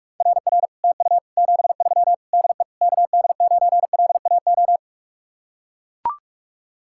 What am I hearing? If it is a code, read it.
Morse code, heard as a keyed tone, about 45 wpm, 700 Hz. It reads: RR TU 73 DE KD9LAO